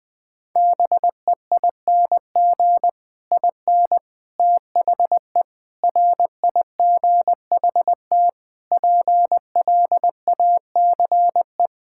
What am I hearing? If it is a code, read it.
Morse code, heard as a keyed tone, about 21 wpm, 700 Hz. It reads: BEING IN THE RIGHT PLACE
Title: Sentences with 9-Word Phrases - Spoken Beginning - 20wpm